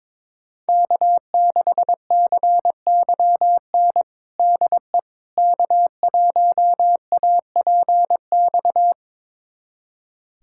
Morse code, heard as a keyed tone, about 22 wpm, 700 Hz. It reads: K6CYN DE K1APX
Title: USA Callsign DE Callsign - 5 Characters - 22wpm